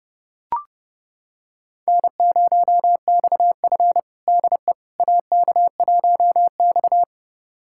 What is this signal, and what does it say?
Morse code, heard as a keyed tone, about 29 wpm, 700 Hz. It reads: N0XF DE AK1X
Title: USA Callsign DE Callsign - 4 Characters - 30wpm